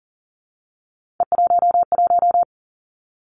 Morse code, heard as a keyed tone, about 41 wpm, 700 Hz. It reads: E11
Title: Three Challenging Letter-Number Combinations 40wpm